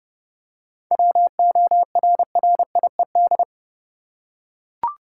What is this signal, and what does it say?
Morse code, heard as a keyed tone, about 30 wpm, 700 Hz. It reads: WORRIED
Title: Seven Letter Words - 30wpm